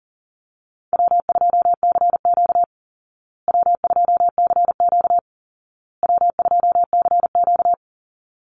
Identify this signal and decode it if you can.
Morse code, heard as a keyed tone, about 40 wpm, 700 Hz. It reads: W2CQ W2CQ W2CQ
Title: Easy Call Signs Repeated 3x - 40wpm